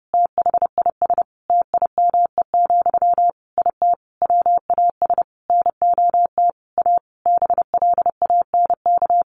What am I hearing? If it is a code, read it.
Morse code, heard as a keyed tone, about 30 wpm, 700 Hz. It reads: THIS TIME, IT WAS NOT A BLANK
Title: Sentences from Top 1500 Words 30wpm